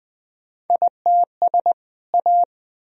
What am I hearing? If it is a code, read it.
Morse code, heard as a keyed tone, about 20 wpm, 700 Hz. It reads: ITS A
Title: Sentences from Top 1000 Words - Review Part 1 - 20wpm